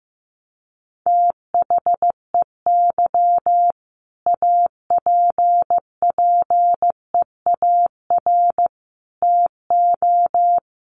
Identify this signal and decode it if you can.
Morse code, heard as a keyed tone, about 15 wpm, 700 Hz. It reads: THEY APPEAR TO
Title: Sentences from Top 1400 Words 15wpm